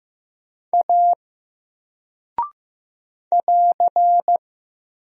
Morse code, heard as a keyed tone, about 15 wpm, 700 Hz. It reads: A <AR>
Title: Single Character (Letter:Number:Punctuation:Prosign) Alt 15wpm